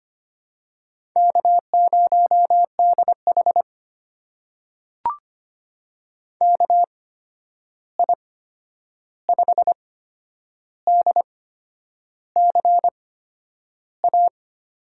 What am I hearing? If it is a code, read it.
Morse code, heard as a keyed tone, about 25 wpm, spaced as for 9 wpm, 700 Hz. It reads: K0DH KI5DCA
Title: USA Call Signs - 8x Character Spacing - 25wpm